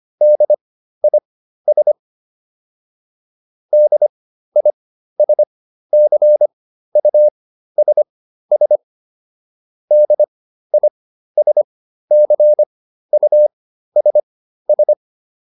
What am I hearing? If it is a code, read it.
Morse code, heard as a keyed tone, about 25 wpm, spaced as for 13 wpm, 600 Hz. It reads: DIS DISCUSS DISCUSS
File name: DIS7 13